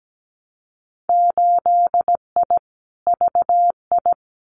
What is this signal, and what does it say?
Morse code, heard as a keyed tone, about 17 wpm, 700 Hz. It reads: 8I VI
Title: Winter Field Day Exchanges Encore 17wpm